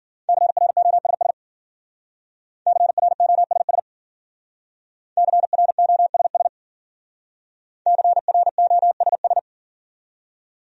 Morse code, heard as a keyed tone, about 51 wpm, 700 Hz. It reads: CROSS CROSS CROSS CROSS